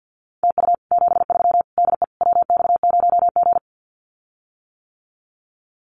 Morse code, heard as a keyed tone, about 50 wpm, 700 Hz. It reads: TU 73 DE WX0G
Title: Common QSO Phrases Encore 50wpm